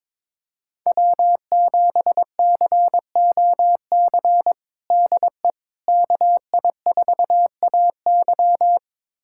Morse code, heard as a keyed tone, about 22 wpm, 700 Hz. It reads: W7COC DE KI4AY
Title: USA Callsign DE Callsign - 5 Characters - 22wpm